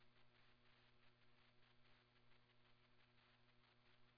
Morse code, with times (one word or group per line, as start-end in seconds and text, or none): none